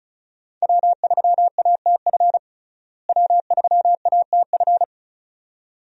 Morse code, heard as a keyed tone, about 34 wpm, 700 Hz. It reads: W3ATF W3ATF
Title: USA Call Signs - 5 Characters - Repeated 2x - 35wpm